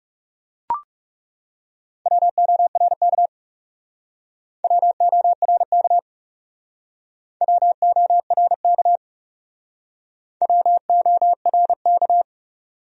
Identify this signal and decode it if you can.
Morse code, heard as a keyed tone, about 40 wpm, 700 Hz. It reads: WORK WORK WORK WORK